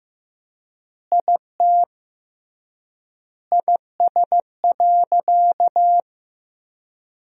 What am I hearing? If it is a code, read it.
Morse code, heard as a keyed tone, about 15 wpm, 700 Hz. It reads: IT IS.
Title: 3x Word-Spacing - Sentences from Top 100 Words - 15wpm